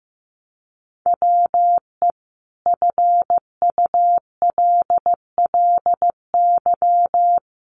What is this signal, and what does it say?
Morse code, heard as a keyed tone, about 15 wpm, 700 Hz. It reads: WE FULLY
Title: Sentences from Top 1300 Words 15wpm